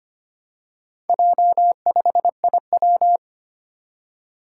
Morse code, heard as a keyed tone, about 25 wpm, 700 Hz. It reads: J5IW
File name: practice / 25wpm / Four Letter-Number Combinations - 25wpm